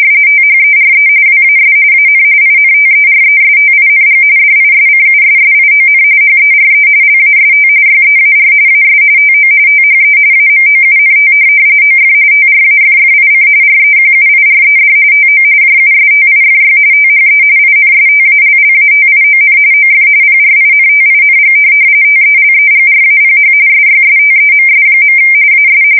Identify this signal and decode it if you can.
RTTY, about 45.45 baud. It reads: THERE WAS AN INTERESTING DISCUSSION HERE LAST YEAR AS TO THE AMOUNT OF TRAFFIC REQUIRED TO JUSTIFY THE INSTALLATION OF PRINTING TELEGRAPH SYSTEMS. AS
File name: fsk170 / printing-telegraph